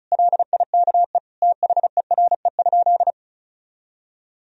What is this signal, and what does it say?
Morse code, heard as a keyed tone, about 36 wpm, 700 Hz. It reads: LIKE THERE?
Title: Sentences from Top 1000 Words - Review Part 4 - 35wpm